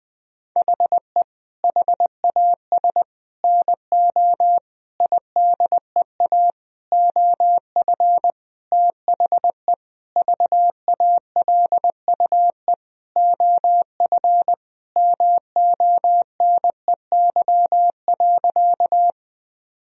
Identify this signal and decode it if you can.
Morse code, heard as a keyed tone, about 20 wpm, 700 Hz. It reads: HE HAS NO IDEA OF THE VALUE OF MONEY.
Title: Sentences from Top 1000 Words - Review Part 3 - 20wpm